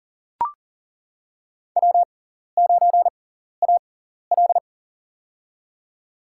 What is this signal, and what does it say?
Morse code, heard as a keyed tone, about 40 wpm, spaced as for 14 wpm, 700 Hz. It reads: W9AL
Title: USA Call Signs - 4 Characters - 6x Character Spacing - 40wpm